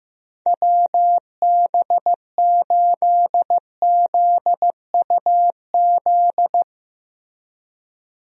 Morse code, heard as a keyed tone, about 16 wpm, 700 Hz. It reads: WB8ZUZ
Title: USA Call Signs - 6 Characters - Repeated 2x - 15wpm